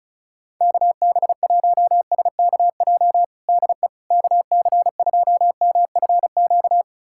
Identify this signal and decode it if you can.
Morse code, heard as a keyed tone, about 34 wpm, 700 Hz. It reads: KB1SKJ DE KC2MFQ
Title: USA Callsign DE Callsign 35wpm